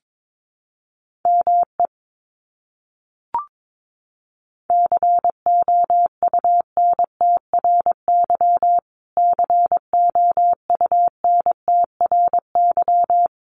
Morse code, heard as a keyed tone, about 22 wpm, 700 Hz. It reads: ME COUNTRY COUNTRY
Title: Top 300 Words Repeated 22wpm